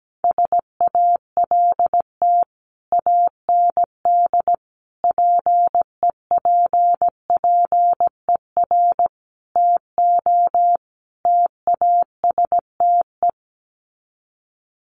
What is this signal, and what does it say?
Morse code, heard as a keyed tone, about 17 wpm, 700 Hz. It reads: SALT AND PEPPER TO TASTE
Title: Sets of 5 Words 17wpm